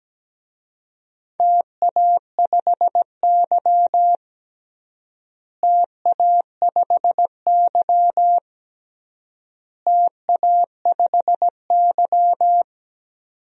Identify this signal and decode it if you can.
Morse code, heard as a keyed tone, about 17 wpm, 700 Hz. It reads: TA5Y TA5Y TA5Y
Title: DXCC-100 Call Signs - 4 Characters - Repeated 3x - 17wpm